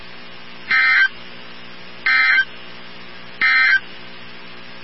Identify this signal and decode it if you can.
Emergency Alert System (SAME) tones, heard as harsh buzzing bursts. It is heard from 0.6 to 3.9 s.
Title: NOAA Weather Radio 162.550